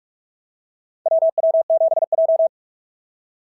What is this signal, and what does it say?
Morse code, heard as a keyed tone, about 44 wpm, 690 Hz. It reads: WW7J
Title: SOTA QSO Practice 45wpm